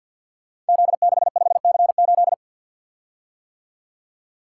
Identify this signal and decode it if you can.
Morse code, heard as a keyed tone, about 51 wpm, 700 Hz. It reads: Z65C8
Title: Code Groups - 5 Letter:Numbers - 50wpm